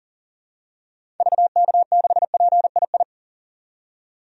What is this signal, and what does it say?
Morse code, heard as a keyed tone, about 37 wpm, 700 Hz. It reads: VK6PII
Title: DXCC-100 Call Signs - 6 Characters - 40wpm